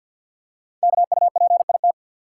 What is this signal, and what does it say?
Morse code, heard as a keyed tone, about 52 wpm, 700 Hz. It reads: KUPIT
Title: Code Groups - 5 Letter:Numbers - 50wpm